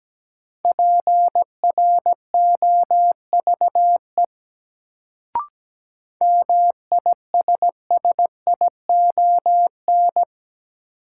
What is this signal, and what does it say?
Morse code, heard as a keyed tone, about 17 wpm, 700 Hz. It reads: PROVE MISSION
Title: Sentences from Top 1600 Words - New Word Drill Alt - 17wpm